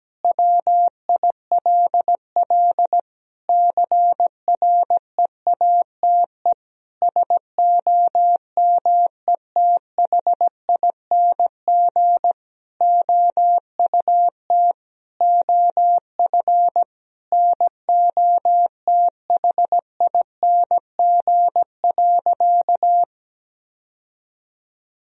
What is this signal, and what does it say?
Morse code, heard as a keyed tone, about 17 wpm, 700 Hz. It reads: WILL CREATE SOMETHING OUT OF NOTHING.
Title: Sentences from Top 1000 Words - Review Part 3 - 17wpm